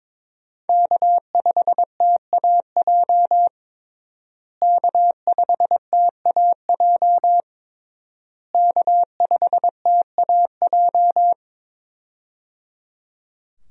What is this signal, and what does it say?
Morse code, heard as a keyed tone, about 22 wpm, 700 Hz. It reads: K5TAJ K5TAJ K5TAJ
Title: USA Call Signs - 5 Characters - Repeated 3x - 22wpm